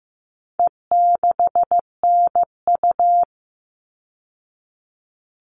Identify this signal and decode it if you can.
Morse code, heard as a keyed tone, about 15 wpm, 700 Hz. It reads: E6NU
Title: All Cut Numbers - 4-Digit 15wpm